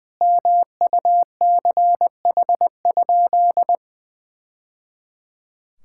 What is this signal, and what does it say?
Morse code, heard as a keyed tone, about 20 wpm, 700 Hz. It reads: MUCH?
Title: Sentences from Top 1000 Words - Review Part 5 - 20wpm